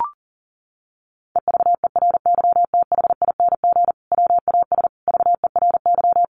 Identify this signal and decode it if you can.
Morse code, heard as a keyed tone, about 37 wpm, 700 Hz. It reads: EVERYTHING WAS VERY
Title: Sentences from Top 1100 Words 40wpm